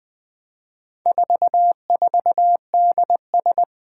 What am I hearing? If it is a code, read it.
Morse code, heard as a keyed tone, about 20 wpm, 700 Hz. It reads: 44DS
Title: Four Challenging Letter-Number Combinations 20wpm